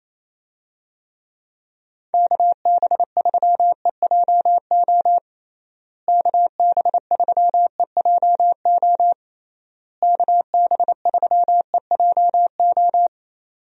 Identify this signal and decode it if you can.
Morse code, heard as a keyed tone, about 27 wpm, 700 Hz. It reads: KB3EJO KB3EJO KB3EJO
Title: USA Call Signs - 6 Characters - Repeated 3x - 28wpm